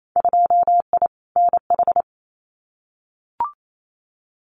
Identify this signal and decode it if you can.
Morse code, heard as a keyed tone, about 28 wpm, 700 Hz. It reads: 2I NH
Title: Winter Field Day Exchanges Encore 28wpm